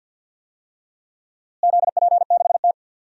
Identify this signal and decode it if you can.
Morse code, heard as a keyed tone, about 52 wpm, 700 Hz. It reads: ZP6T
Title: DXCC-100 Call Signs - 4 Characters - Repeated 2x - 50wpm